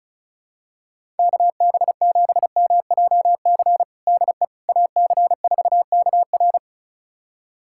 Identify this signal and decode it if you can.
Morse code, heard as a keyed tone, about 34 wpm, 700 Hz. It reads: KB7MJC DE AC4KR
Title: USA Callsign DE Callsign 35wpm